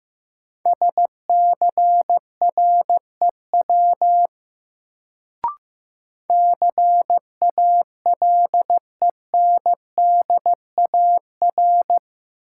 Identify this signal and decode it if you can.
Morse code, heard as a keyed tone, about 15 wpm, 700 Hz. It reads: SCREW CALENDAR
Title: Sentences from Top 1600 Words - New Word Drill Alt - 15wpm